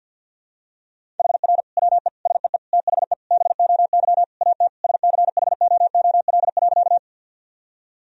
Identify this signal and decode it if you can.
Morse code, heard as a keyed tone, about 46 wpm, 700 Hz. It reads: HR WE SEE THE BOY AT SCHOOL.
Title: Sentences from Top 300 Words and Abbreviations 50wpm